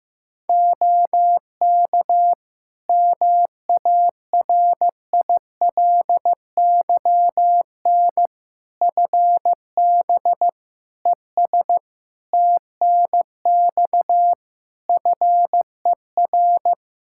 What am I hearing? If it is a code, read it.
Morse code, heard as a keyed tone, about 15 wpm, 700 Hz. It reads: OK MARILYN FB ES TNX FER
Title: Practice QSO - Exchange 2 - Info - 15wpm